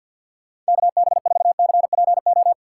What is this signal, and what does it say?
Morse code, heard as a keyed tone, about 61 wpm, 700 Hz. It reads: KB4CPO
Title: Speed Racing - USA 2x3 Call Signs 50wpm